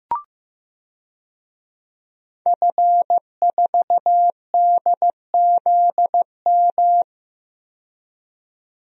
Morse code, heard as a keyed tone, about 15 wpm, 700 Hz. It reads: F4DZM
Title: DXCC-100 Call Signs - 5 Characters - 15wpm